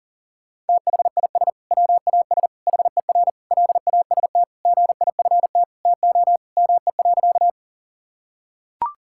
Morse code, heard as a keyed tone, about 37 wpm, 700 Hz. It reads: THIS WAS HER LAST GIFT TO ME.
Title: Sentences from Top 1000 Words - Review Part 6 - 40wpm